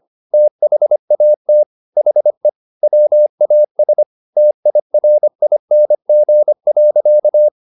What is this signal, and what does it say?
Morse code, heard as a keyed tone, about 25 wpm, 600 Hz. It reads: THAT HE WAS TIRING.
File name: the old man and the sea part 1 25wpm